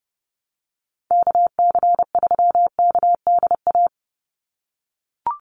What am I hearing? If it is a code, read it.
Morse code, heard as a keyed tone, about 30 wpm, 700 Hz. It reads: KC3KDA
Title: USA Call Signs 30wpm